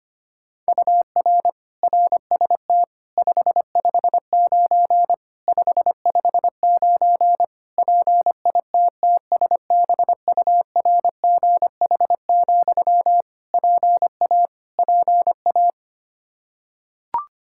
Morse code, heard as a keyed tone, about 25 wpm, 700 Hz. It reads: UR RST 559 559 PITTSBURGH, PA PA
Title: Practice QSO - Exchange 1 - Beginning - 25wpm